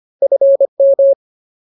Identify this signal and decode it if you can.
Morse code, heard as a keyed tone, about 25 wpm, 550 Hz. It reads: FM
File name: Ultimate Word List / 25WPM Ultimate